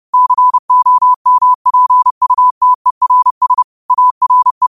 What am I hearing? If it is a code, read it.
Morse code, heard as a keyed tone, about 30 wpm, 1000 Hz. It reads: COMPUTERS ARE
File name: qotdfiles / 30wpm